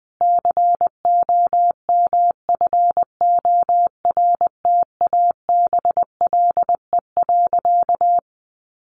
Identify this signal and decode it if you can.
Morse code, heard as a keyed tone, about 20 wpm, 700 Hz. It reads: COMFORTABLE.
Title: Sentences from Top 1500 Words 20wpm